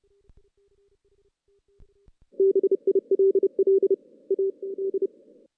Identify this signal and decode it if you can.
Morse code, heard as a keyed tone, about 32 wpm, 390 Hz. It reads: BILL A D
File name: 2016 JAN CW NAQP / 2016 Jan naqp k5ot - 0-6